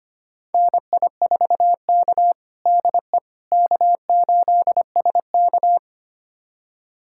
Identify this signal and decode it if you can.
Morse code, heard as a keyed tone, about 25 wpm, 700 Hz. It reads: NI4K DE K8SK
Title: USA Callsign DE Callsign - 4 Characters - 25wpm